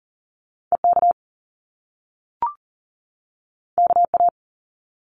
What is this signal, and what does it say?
Morse code, heard as a keyed tone, about 38 wpm, 700 Hz. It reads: EK KA
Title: Top Two Letter Combinations in Words 40wpm